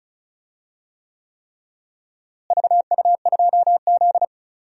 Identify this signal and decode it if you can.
Morse code, heard as a keyed tone, about 36 wpm, 700 Hz. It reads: VU2Z
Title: DXCC-100 Call Signs - 4 Characters - Repeated 3x - 35wpm